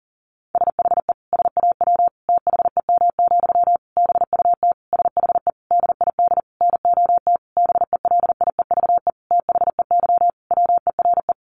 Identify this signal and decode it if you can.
Morse code, heard as a keyed tone, about 40 wpm, 700 Hz. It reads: SHE SAW THEM, BUT SHE DID NOT BELIEVE THEY WERE
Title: Sentences from Top 1000 Words 40wpm